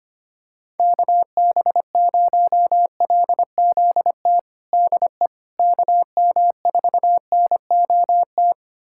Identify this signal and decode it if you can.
Morse code, heard as a keyed tone, about 25 wpm, 700 Hz. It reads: KB0LZT DE KM4NOT
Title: USA Callsign DE Callsign - 6 Characters - 25wpm